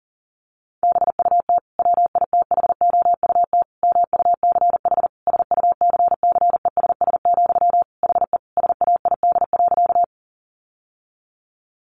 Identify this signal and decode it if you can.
Morse code, heard as a keyed tone, about 40 wpm, 700 Hz. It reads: BUT WITHOUT MUCH SUCCESS, HE SAID.
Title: Sentences from Top 900 Words 40wpm